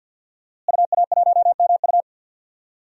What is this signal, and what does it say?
Morse code, heard as a keyed tone, about 52 wpm, 700 Hz. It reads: UA1MU
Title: DXCC-100 Call Signs - 5 Characters - Repeated 3x - 50wpm